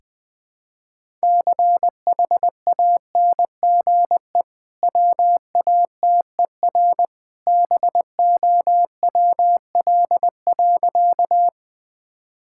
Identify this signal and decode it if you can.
Morse code, heard as a keyed tone, about 21 wpm, 700 Hz. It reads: CHANGE WATER BOWL.
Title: Easy Sentences with Hints - 20wpm